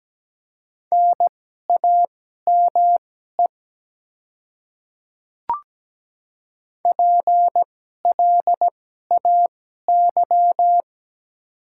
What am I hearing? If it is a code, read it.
Morse code, heard as a keyed tone, about 17 wpm, spaced as for 12 wpm, 700 Hz. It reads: NAME PLAY
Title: 2x Character Spacing - Top 200 Words - 17wpm